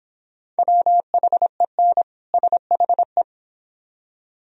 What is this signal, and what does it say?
Morse code, heard as a keyed tone, about 25 wpm, 700 Hz. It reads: WHEN SHE